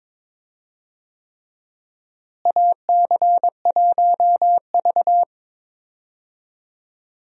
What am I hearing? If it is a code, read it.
Morse code, heard as a keyed tone, about 22 wpm, 700 Hz. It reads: AC1V
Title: USA Call Signs - 4 Characters - Repeated 3x - 22wpm